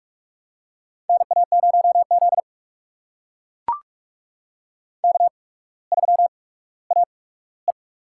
Morse code, heard as a keyed tone, about 44 wpm, spaced as for 19 wpm, 700 Hz. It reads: NA0Z K3AE
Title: USA Call Signs - 4 Characters - 8x Character Spacing - 45wpm